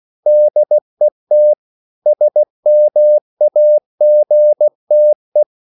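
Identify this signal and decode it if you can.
Morse code, heard as a keyed tone, about 16 wpm, 600 Hz. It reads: DET SMAGTE
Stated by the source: Slow (QRS) Morse code at 16 wpm, from CW med Gnister.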